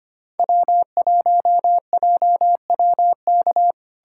Morse code, heard as a keyed tone, about 25 wpm, 700 Hz. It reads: W1JWK
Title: USA Callsign DE Callsign - 5 Characters - 25wpm